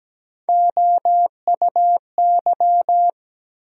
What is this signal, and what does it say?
Morse code, heard as a keyed tone, about 17 wpm, 700 Hz. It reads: OUY
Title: USA Call Sign Postfixes - 3 Letters 17wpm